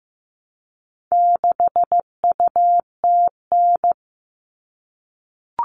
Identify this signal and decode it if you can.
Morse code, heard as a keyed tone, about 15 wpm, 700 Hz. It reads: 6UTN
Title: All Cut Numbers - 4-Digit 15wpm